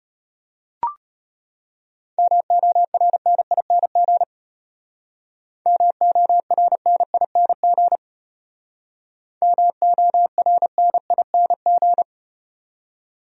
Morse code, heard as a keyed tone, about 35 wpm, 700 Hz. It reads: MORNING MORNING MORNING